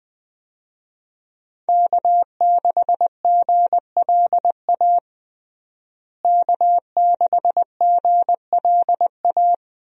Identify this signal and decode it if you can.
Morse code, heard as a keyed tone, about 21 wpm, 700 Hz. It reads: K6GLA K6GLA
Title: USA Call Signs - 5 Characters - Repeated 3x - 20wpm